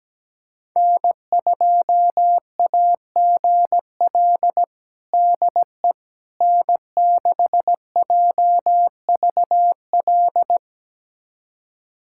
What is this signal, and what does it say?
Morse code, heard as a keyed tone, about 17 wpm, 700 Hz. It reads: N2AGL DE N6JVL
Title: USA Callsign DE Callsign - 5 Characters - 17wpm